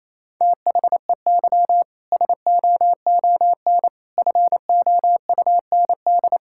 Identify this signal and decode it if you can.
Morse code, heard as a keyed tone, about 28 wpm, 700 Hz. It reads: THEY SOON FOUND
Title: Sentences from Top 1000 Words - Review Part 5 - 28wpm